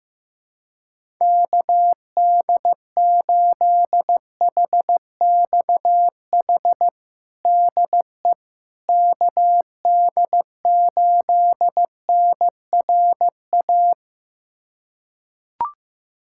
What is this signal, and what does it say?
Morse code, heard as a keyed tone, about 15 wpm, 700 Hz. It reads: KD8HXH DE KD8NRA
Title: USA Callsign DE Callsign 15wpm